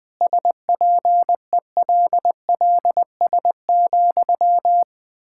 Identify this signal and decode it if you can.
Morse code, heard as a keyed tone, about 20 wpm, 700 Hz. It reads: SPELLS,
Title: Pangram Challenge v2 - 20wpm